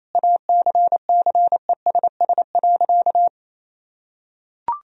Morse code, heard as a keyed tone, about 28 wpm, 700 Hz. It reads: ACCESS.
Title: Sentences from Top 1000 Words - Review Part 4 - 28wpm